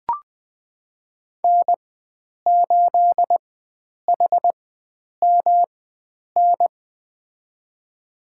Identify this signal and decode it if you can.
Morse code, heard as a keyed tone, about 20 wpm, spaced as for 9 wpm, 700 Hz. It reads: N8HMN
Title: USA Call Signs - 5 Characters - 4x Character Spacing - 20wpm